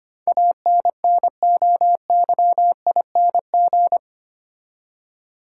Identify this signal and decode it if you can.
Morse code, heard as a keyed tone, about 25 wpm, 700 Hz. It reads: ANNOYING
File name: Eight Letter Words - 25wpm